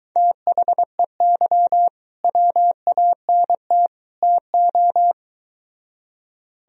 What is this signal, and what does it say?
Morse code, heard as a keyed tone, about 22 wpm, 700 Hz. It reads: THEY WANT TO